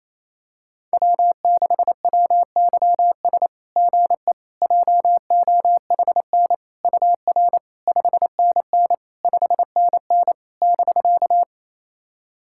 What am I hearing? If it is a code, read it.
Morse code, heard as a keyed tone, about 28 wpm, 700 Hz. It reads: W6WYS GE JOHN UR 5NN 5NN <BK>
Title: SOTA QSO Practice 28wpm